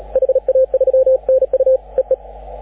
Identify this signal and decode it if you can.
Morse code, heard as a keyed tone, about 35 wpm, 530 Hz, with radio noise. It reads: HA3NU EE
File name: AUDIO2019 / HG6N CW SSB MIX